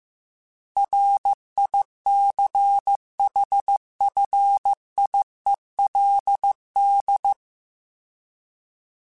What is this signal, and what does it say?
Morse code, heard as a keyed tone, about 15 wpm, 800 Hz. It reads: RICHFIELD
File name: fromSMs / FromSMs mp3s / Cities@13